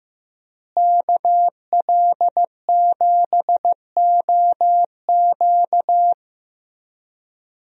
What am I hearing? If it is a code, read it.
Morse code, heard as a keyed tone, about 15 wpm, 700 Hz. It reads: KL7OQ
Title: DXCC-100 Call Signs - 5 Characters - Repeated 3x - 15wpm